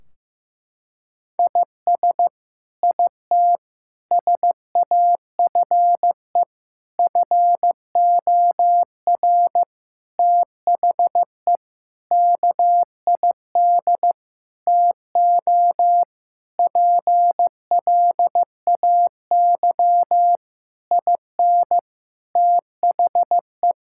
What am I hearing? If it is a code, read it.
Morse code, heard as a keyed tone, about 15 wpm, 700 Hz. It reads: IS IT SAFE FOR THE KID TO PLAY IN THE
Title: Sentences from Top 1600 Words 15wpm